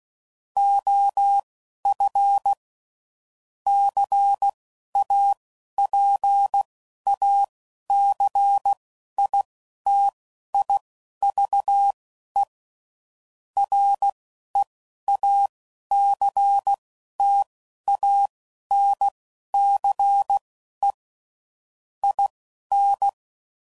Morse code, heard as a keyed tone, about 16 wpm, spaced as for 12 wpm, 800 Hz. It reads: OF CAPACITIVE REACTANCE IN
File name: CW Code Groups @10WPM / Technical Subjects / XC